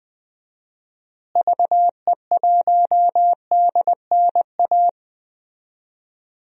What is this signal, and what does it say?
Morse code, heard as a keyed tone, about 20 wpm, 700 Hz. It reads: VE1DNA
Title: DXCC-100 Call Signs - 6 Characters - Repeated 3x - 20wpm